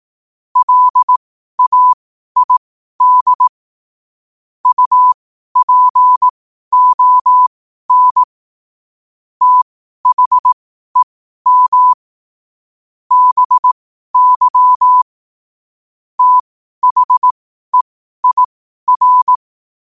Morse code, heard as a keyed tone, about 18 wpm, spaced as for 12 wpm, 1000 Hz. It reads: LAID UPON THEM BY THEIR
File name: qotdfiles / 12wpm